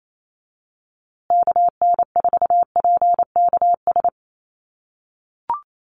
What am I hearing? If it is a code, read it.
Morse code, heard as a keyed tone, about 28 wpm, 700 Hz. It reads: KN4PKS
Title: USA Call Signs Repeated 2x 28wpm